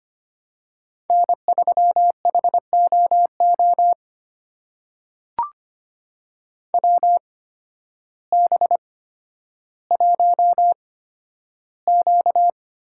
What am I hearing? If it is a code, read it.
Morse code, heard as a keyed tone, about 25 wpm, spaced as for 12 wpm, 700 Hz. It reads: N3HOO WB1Q
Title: USA Call Signs - 8x Character Spacing - 25wpm